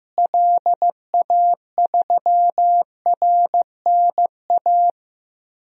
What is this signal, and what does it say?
Morse code, heard as a keyed tone, about 15 wpm, 700 Hz. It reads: LA3RNA
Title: DXCC-100 Call Signs - 6 Characters - Repeated 2x - 15wpm